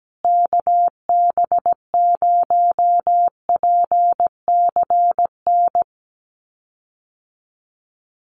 Morse code, heard as a keyed tone, about 17 wpm, 700 Hz. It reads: KB0PCN
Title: USA Call Signs Repeated 2x 17wpm